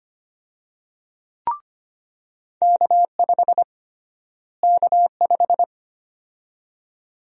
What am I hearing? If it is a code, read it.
Morse code, heard as a keyed tone, about 25 wpm, 700 Hz. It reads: K5 K5
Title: USA Call Sign Prefixes - Repeated 2x - 25wpm